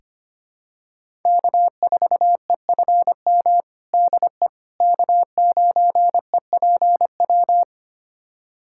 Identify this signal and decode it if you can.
Morse code, heard as a keyed tone, about 25 wpm, 700 Hz. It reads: K4EFM DE K9EPW
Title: USA Callsign DE Callsign - 5 Characters - 25wpm